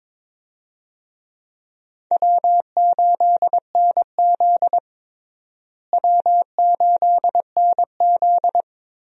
Morse code, heard as a keyed tone, about 22 wpm, 700 Hz. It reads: W8NZ W8NZ
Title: USA Call Signs - 4 Characters - Repeated 3x - 22wpm